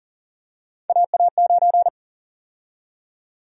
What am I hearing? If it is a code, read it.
Morse code, heard as a keyed tone, about 40 wpm, 700 Hz. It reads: AA9
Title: USA Call Sign Prefixes - 40wpm